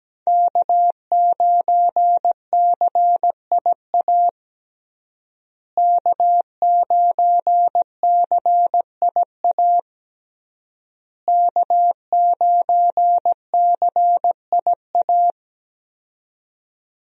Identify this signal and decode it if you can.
Morse code, heard as a keyed tone, about 17 wpm, 700 Hz. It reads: K9CIA K9CIA K9CIA
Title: USA Call Signs - 5 Characters - Repeated 3x - 17wpm